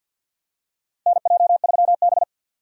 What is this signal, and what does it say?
Morse code, heard as a keyed tone, about 53 wpm, 700 Hz. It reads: NJ3B
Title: USA Call Signs - 4 Characters - Repeated 3x - 50wpm